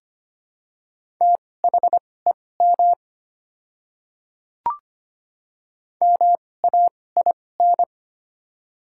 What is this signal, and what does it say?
Morse code, heard as a keyed tone, about 25 wpm, spaced as for 18 wpm, 700 Hz. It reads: THEM MAIN
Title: 2x Character Spacing - Top 300 Words - 25wpm